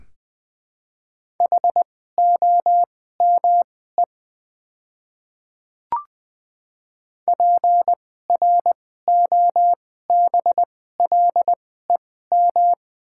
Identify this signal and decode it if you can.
Morse code, heard as a keyed tone, about 21 wpm, spaced as for 15 wpm, 700 Hz. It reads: HOME PROBLEM
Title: 2x Character Spacing - Top 500 Words - 20wpm